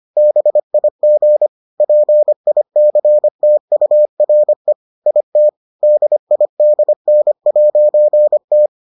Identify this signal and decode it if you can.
Morse code, heard as a keyed tone, about 25 wpm, 600 Hz. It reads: BIG PICTURE IT DIDN'T